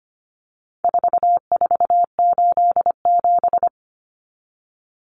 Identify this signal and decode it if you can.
Morse code, heard as a keyed tone, about 25 wpm, 700 Hz. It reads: 4487
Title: Numbers - 4-Digit 25wpm